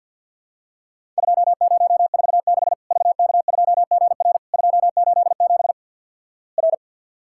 Morse code, heard as a keyed tone, about 49 wpm, 700 Hz. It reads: 2046 VK2GR 287 R
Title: Worked All Europe DX Contest - QTC Exchange - 50wpm